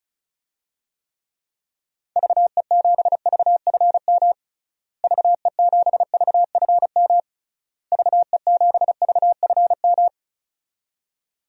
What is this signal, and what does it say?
Morse code, heard as a keyed tone, about 34 wpm, 700 Hz. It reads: VE7VFM VE7VFM VE7VFM
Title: DXCC-100 Call Signs - 6 Characters - Repeated 3x - 35wpm